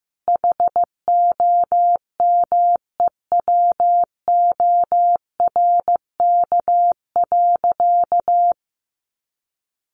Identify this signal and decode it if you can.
Morse code, heard as a keyed tone, about 15 wpm, 700 Hz. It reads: HOMEWORK.
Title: Additional Easy Phrases 15wpm